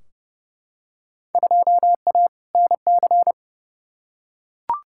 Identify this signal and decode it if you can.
Morse code, heard as a keyed tone, about 32 wpm, 700 Hz. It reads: 2A NC
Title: Field Day Exchanges - 30wpm